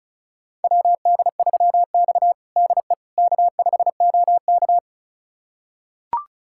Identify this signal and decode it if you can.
Morse code, heard as a keyed tone, about 34 wpm, 690 Hz. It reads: WD3X DE K5OK
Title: USA Callsign DE Callsign - 4 Characters - 35wpm